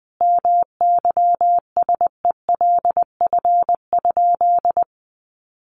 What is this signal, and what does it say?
Morse code, heard as a keyed tone, about 20 wpm, 700 Hz. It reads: MYSELF?
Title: Sentences from Top 1300 Words 20wpm